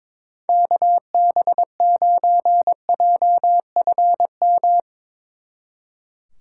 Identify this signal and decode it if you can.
Morse code, heard as a keyed tone, about 22 wpm, 700 Hz. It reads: KB9JFM
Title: Speed Racing - USA Call Signs 22wpm